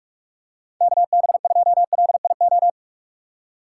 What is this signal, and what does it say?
Morse code, heard as a keyed tone, about 44 wpm, 700 Hz. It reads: KB2LIO
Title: USA Call Signs - 3x Character Spacing - 45wpm